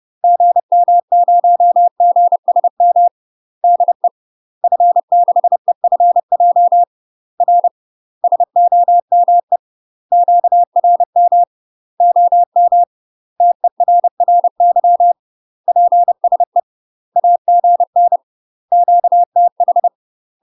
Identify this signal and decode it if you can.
Morse code, heard as a keyed tone, about 30 wpm, 700 Hz. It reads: GM0GSM DE F6EFJ R SOME QRM OM TERRY PSE AGN QTH
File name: UK MASTER QSOs TRAINING / 30WPM / UK MASTER QSOs 0015